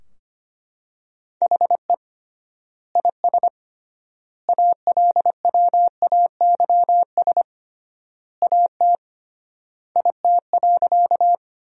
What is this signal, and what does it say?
Morse code, heard as a keyed tone, about 25 wpm, 700 Hz. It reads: HE IS ALWAYS AT IT.
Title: 3x Word-Spacing - Sentences from Top 300 Words - 25wpm